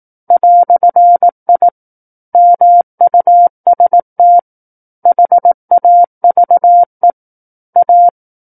Morse code, heard as a keyed tone, about 18 wpm, 700 Hz. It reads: "I MUST HAVE A
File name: 18 WPM